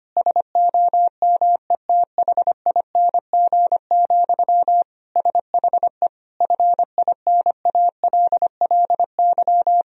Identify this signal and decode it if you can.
Morse code, heard as a keyed tone, about 25 wpm, 700 Hz. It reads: SOMETHING, SHE FINALLY
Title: Sentences from Top 1000 Words - Review Part 3 - 25wpm